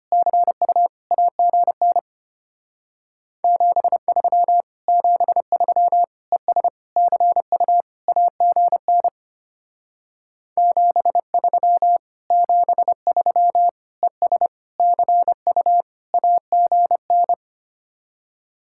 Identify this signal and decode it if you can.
Morse code, heard as a keyed tone, about 31 wpm, 700 Hz. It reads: CU AGN 73 73 ES CU AGN 73 73 ES CU AGN
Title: Speed-Racing Common QSO Phrases - 25wpm